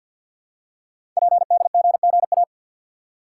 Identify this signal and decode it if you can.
Morse code, heard as a keyed tone, about 52 wpm, 700 Hz. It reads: PDGGA
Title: Code Groups - 5 Letters - 50wpm